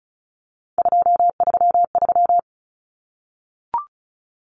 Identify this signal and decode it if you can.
Morse code, heard as a keyed tone, about 34 wpm, 700 Hz. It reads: 233
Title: Numbers - 3-Digit 35wpm